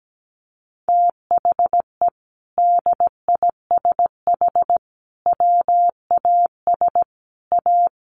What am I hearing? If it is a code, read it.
Morse code, heard as a keyed tone, about 17 wpm, 700 Hz. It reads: THE DISH WAS A
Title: Sentences from Top 1400 Words 17wpm